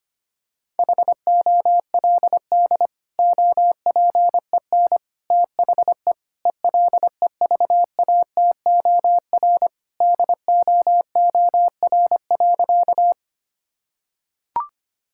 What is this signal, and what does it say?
Morse code, heard as a keyed tone, about 25 wpm, 700 Hz. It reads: HOLD OPEN THE ELEVATOR DOOR.
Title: Sentences from Top 1600 Words 25wpm